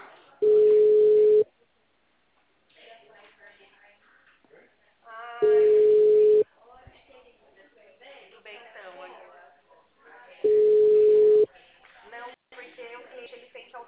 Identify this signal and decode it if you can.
Ringback tone (Europe, 425 Hz), with speech.